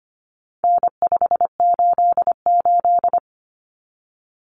Morse code, heard as a keyed tone, about 25 wpm, 700 Hz. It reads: N588
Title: Common Cut Numbers - 4-Digit 25wpm